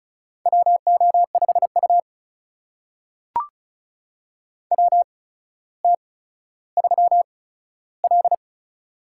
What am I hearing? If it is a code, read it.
Morse code, heard as a keyed tone, about 36 wpm, spaced as for 15 wpm, 700 Hz. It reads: WO5U WT3L
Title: USA Call Signs - 4 Characters - 8x Character Spacing - 35wpm